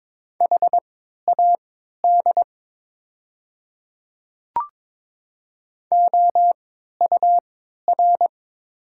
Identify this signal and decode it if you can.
Morse code, heard as a keyed tone, about 22 wpm, spaced as for 12 wpm, 700 Hz. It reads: HAD OUR
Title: 3x Character Spacing - Top 200 Words - 22wpm